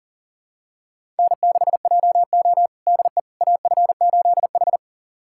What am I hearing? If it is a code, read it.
Morse code, heard as a keyed tone, about 40 wpm, 700 Hz. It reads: N6JO DE AF8H
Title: USA Callsign DE Callsign - 4 Characters - 40wpm